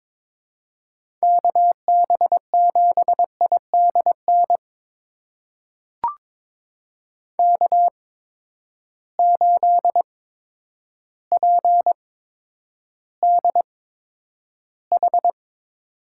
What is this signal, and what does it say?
Morse code, heard as a keyed tone, about 22 wpm, spaced as for 10 wpm, 700 Hz. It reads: KB7IDN K8PDH
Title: USA Call Signs - 8x Character Spacing - 22wpm